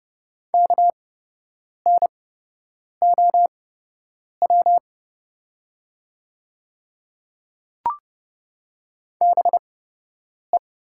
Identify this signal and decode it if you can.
Morse code, heard as a keyed tone, about 30 wpm, spaced as for 8 wpm, 700 Hz. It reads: KNOW BE
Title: 8x Character Spacing - Top 300 Words - 30wpm